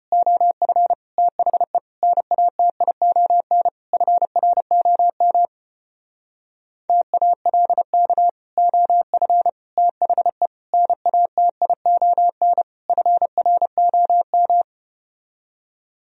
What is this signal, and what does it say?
Morse code, heard as a keyed tone, about 30 wpm, 700 Hz. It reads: OF THE NATION FROM TALK OF THE NATION FROM
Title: Speed-Racing - Sets of 5 Words 25wpm